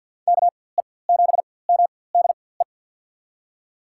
Morse code, heard as a keyed tone, about 49 wpm, spaced as for 23 wpm, 700 Hz. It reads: KE7MDE
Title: USA Call Signs - 4x Character Spacing - 50wpm